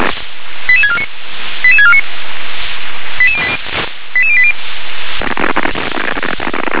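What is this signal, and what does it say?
Five-tone selective call (ZVEI), heard as quick runs of five tones, tones of about 70 ms. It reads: 88649 88649 88080
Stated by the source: Received on an SDR as a ZVEI five-tone call.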